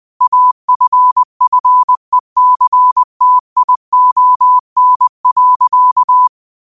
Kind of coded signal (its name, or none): Morse code